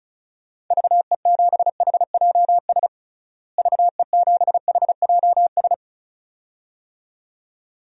Morse code, heard as a keyed tone, about 36 wpm, 700 Hz. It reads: VE7HJS VE7HJS
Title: DXCC-100 Call Signs - 6 Characters - Repeated 2x - 35wpm